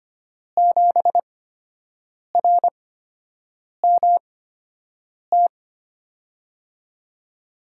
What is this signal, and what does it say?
Morse code, heard as a keyed tone, about 25 wpm, spaced as for 7 wpm, 700 Hz. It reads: 7RMT
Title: USA Call Signs - 5 Characters - 8x Character Spacing - 25wpm